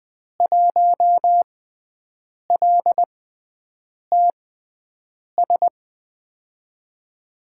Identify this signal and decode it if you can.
Morse code, heard as a keyed tone, about 20 wpm, spaced as for 7 wpm, 700 Hz. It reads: 1LTS
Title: USA Call Signs - 6x Character Spacing - 20wpm